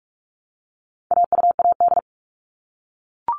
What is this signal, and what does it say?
Morse code, heard as a keyed tone, about 44 wpm, 700 Hz. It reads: AUAD
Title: All Cut Numbers - 4-Digit 45wpm